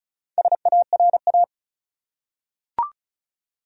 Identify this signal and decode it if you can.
Morse code, heard as a keyed tone, about 36 wpm, 700 Hz. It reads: SARA
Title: USA Common Names - Max 5 Letters - 35wpm